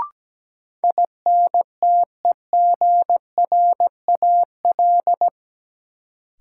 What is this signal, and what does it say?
Morse code, heard as a keyed tone, about 17 wpm, 700 Hz. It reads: INTEGRAL
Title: Sets of 5 Words - New Word Drill - 17wpm